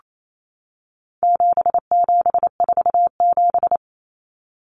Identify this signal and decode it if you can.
Morse code, heard as a keyed tone, about 28 wpm, 700 Hz. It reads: 7747
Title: Common Cut Numbers - 4-Digit 28wpm